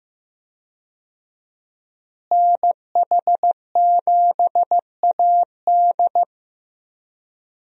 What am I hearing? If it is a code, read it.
Morse code, heard as a keyed tone, about 15 wpm, 700 Hz. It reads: NH7AD
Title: USA Call Signs - 5 Characters - Repeated 3x - 15wpm